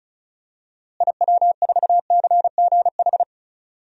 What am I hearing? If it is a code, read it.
Morse code, heard as a keyed tone, about 36 wpm, 700 Hz. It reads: IW4CGH
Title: DXCC-100 Call Signs - 6 Characters - Repeated 3x - 35wpm